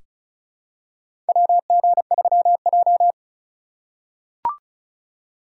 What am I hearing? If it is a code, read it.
Morse code, heard as a keyed tone, about 34 wpm, 700 Hz. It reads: WG3J
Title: USA Call Signs - 4 Characters - 3x Character Spacing - 35wpm